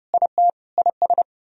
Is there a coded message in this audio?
Yes: Morse code from 0.1 to 1.2 s, heard as a keyed tone, about 40 wpm, 700 Hz. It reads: IT IS